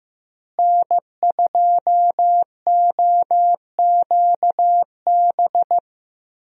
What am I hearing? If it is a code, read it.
Morse code, heard as a keyed tone, about 15 wpm, 700 Hz. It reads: N2OQB
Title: USA Call Signs - 5 Characters - Repeated 3x - 15wpm